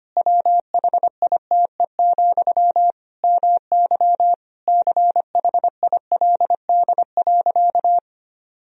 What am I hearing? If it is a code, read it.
Morse code, heard as a keyed tone, about 25 wpm, 700 Hz. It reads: WHITE, MY CHILD.
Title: Sentences from Top 1000 Words - Review Part 1 - 25wpm